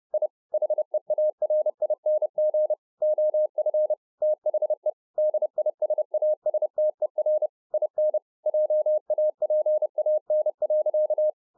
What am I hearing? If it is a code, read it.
Morse code, heard as a keyed tone, about 30 wpm, 610 Hz. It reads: I HEARING OF THE DISASTER IN JAPAN.